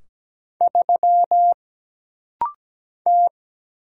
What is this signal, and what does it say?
Morse code, heard as a keyed tone, about 17 wpm, 700 Hz. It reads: 3 T
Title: Single Letter-Number - Rapid-Fire 17wpm